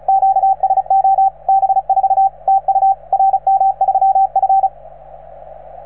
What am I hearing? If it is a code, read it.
Morse code, heard as a keyed tone, about 36 wpm, 770 Hz, with radio noise. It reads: QSO B4 TU RM3F